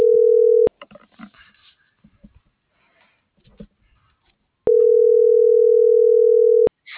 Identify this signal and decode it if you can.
Ringback tone (North America), 440 + 480 Hz.